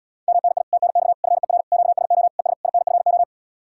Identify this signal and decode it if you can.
Morse code, heard as a keyed tone, about 78 wpm, 700 Hz. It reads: MNI TNX FER QSO ES INFO